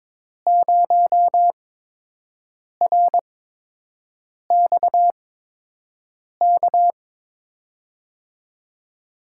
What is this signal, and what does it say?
Morse code, heard as a keyed tone, about 22 wpm, spaced as for 6 wpm, 700 Hz. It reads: 0RXK
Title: USA Call Signs - 8x Character Spacing - 22wpm